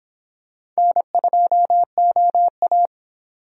Morse code, heard as a keyed tone, about 26 wpm, 700 Hz. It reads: N2OA